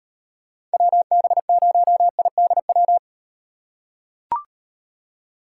Morse code, heard as a keyed tone, about 37 wpm, 700 Hz. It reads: WB0IDW